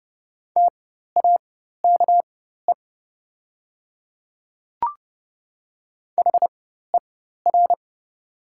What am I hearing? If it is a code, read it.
Morse code, heard as a keyed tone, about 30 wpm, spaced as for 14 wpm, 700 Hz. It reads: TAKE HER
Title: 4x Character Spacing - Top 200 Words - 30wpm